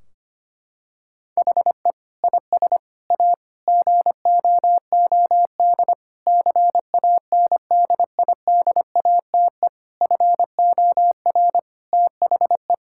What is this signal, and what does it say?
Morse code, heard as a keyed tone, about 25 wpm, 700 Hz. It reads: HE IS A GOOD CANDIDATE FOR THE
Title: Sentences from Top 1600 Words 25wpm